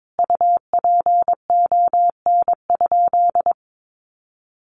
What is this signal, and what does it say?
Morse code, heard as a keyed tone, about 22 wpm, 700 Hz. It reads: UPON?
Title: Sentences from Top 1100 Words 22wpm